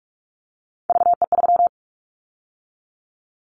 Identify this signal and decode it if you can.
Morse code, heard as a keyed tone, about 44 wpm, 700 Hz. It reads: VE3
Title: Callsign Prefixes for Top 100 DXCC - 45wpm